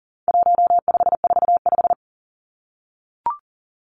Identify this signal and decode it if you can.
Morse code, heard as a keyed tone, about 40 wpm, 700 Hz. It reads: 1545
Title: Numbers - 4-Digit 40wpm